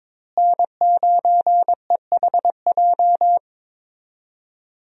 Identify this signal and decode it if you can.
Morse code, heard as a keyed tone, about 22 wpm, 700 Hz. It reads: N9EHJ
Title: USA Call Signs - 5 Characters - Repeated 2x - 22wpm